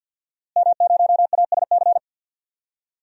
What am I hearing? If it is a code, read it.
Morse code, heard as a keyed tone, about 51 wpm, 700 Hz. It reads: M0ASC
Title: DXCC-100 Call Signs - 5 Characters - Repeated 2x - 50wpm